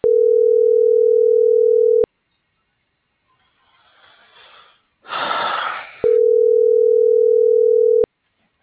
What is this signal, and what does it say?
Ringback tone (North America), 440 + 480 Hz.
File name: CENTRAL MABE / out cristopher.carrera